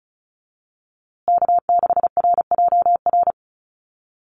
Morse code, heard as a keyed tone, about 34 wpm, 700 Hz. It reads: K6RJR
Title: USA Call Signs Repeated 2x 35wpm